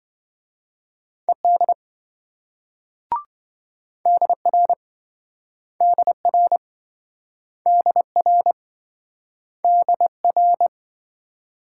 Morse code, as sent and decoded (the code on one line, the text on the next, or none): . -.. / -.. .-. / -.. .-. / -.. .-. / -.. .-.
ED DR DR DR DR